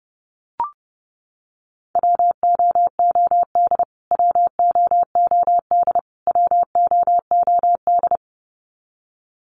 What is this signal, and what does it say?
Morse code, heard as a keyed tone, about 28 wpm, 700 Hz. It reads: WOOD WOOD WOOD
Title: Top 500 Words Repeated 30wpm